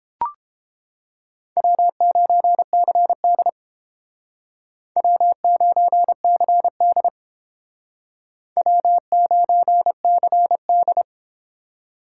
Morse code, heard as a keyed tone, about 30 wpm, 700 Hz. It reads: W9CD W9CD W9CD